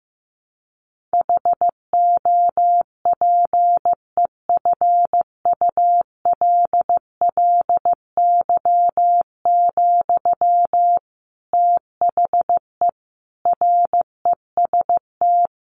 Morse code, heard as a keyed tone, about 14 wpm, 700 Hz. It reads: HOPEFULLY, THE REST
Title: Sentences from Top 1500 Words 15wpm